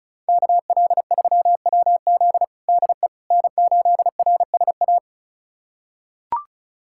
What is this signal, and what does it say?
Morse code, heard as a keyed tone, about 36 wpm, 700 Hz. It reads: KL3WZ DE N8RSA
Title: USA Callsign DE Callsign - 5 Characters - 35wpm